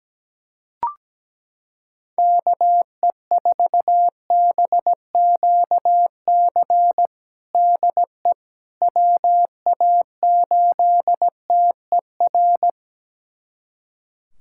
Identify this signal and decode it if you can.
Morse code, heard as a keyed tone, about 17 wpm, 700 Hz. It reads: KE4BQC DE WA8TER
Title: USA Callsign DE Callsign 17wpm